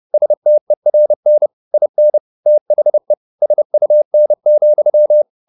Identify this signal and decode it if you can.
Morse code, heard as a keyed tone, about 30 wpm, 600 Hz. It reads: STERN IN THE SUN,
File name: the old man and the sea / the old man and the sea 1 30wpm